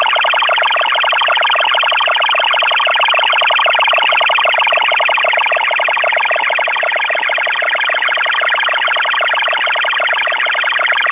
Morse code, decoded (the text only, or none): none